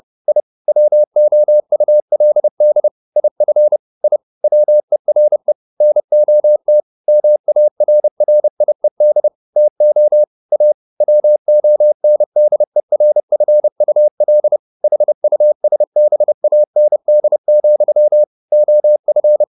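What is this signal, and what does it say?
Morse code, heard as a keyed tone, about 30 wpm, 600 Hz. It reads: I WOULD IF I WERE NOT MARRIED TO A WONDERFUL HUSBAND, OF
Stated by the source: Morse code at 30 wpm, the novel Never by Ken Follett in English, from CW med Gnister.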